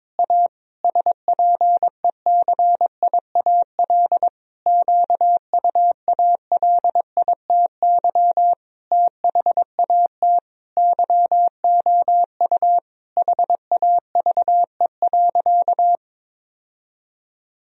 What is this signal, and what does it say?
Morse code, heard as a keyed tone, about 21 wpm, 700 Hz. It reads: A SPECIAL QUALITY THAT YOU HAVE.
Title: Sentences from Top 1000 Words - Review Part 4 - 22wpm